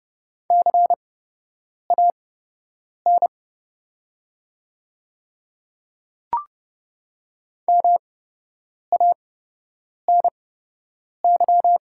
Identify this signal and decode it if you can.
Morse code, heard as a keyed tone, about 29 wpm, spaced as for 8 wpm, 700 Hz. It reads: CAN MANY
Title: 8x Character Spacing - Top 100 Words - 30wpm